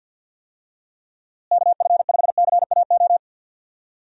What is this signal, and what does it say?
Morse code, heard as a keyed tone, about 52 wpm, 700 Hz. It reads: KF5CAO
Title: USA Call Signs - 6 Characters - Repeated 3x - 50wpm